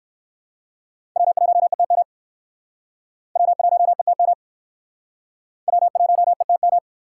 Morse code, heard as a keyed tone, about 60 wpm, 700 Hz. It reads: W1ETM W1ETM W1ETM